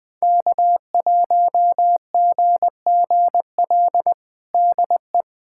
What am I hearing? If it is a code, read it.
Morse code, heard as a keyed tone, about 20 wpm, 700 Hz. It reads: K1GGL DE
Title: USA Callsign DE Callsign - 5 Characters - 20wpm